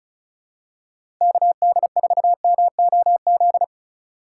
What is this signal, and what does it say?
Morse code, heard as a keyed tone, about 36 wpm, 700 Hz. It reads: KD4MOZ
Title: USA Call Signs - 6 Characters - 4x Character Spacing - 35wpm